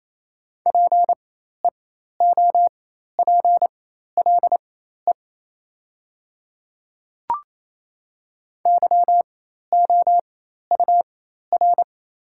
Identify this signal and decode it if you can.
Morse code, heard as a keyed tone, about 28 wpm, spaced as for 13 wpm, 700 Hz. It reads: PEOPLE YOUR
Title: 4x Character Spacing - Top 100 Words - 28wpm